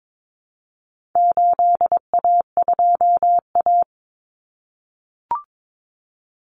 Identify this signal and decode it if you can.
Morse code, heard as a keyed tone, about 22 wpm, 700 Hz. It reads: 8A2A
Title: Common Cut Numbers - 4-Digit 22wpm